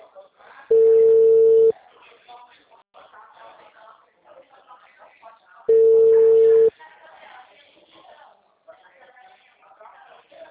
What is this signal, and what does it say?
Ringback tone (China, 450 Hz), with speech.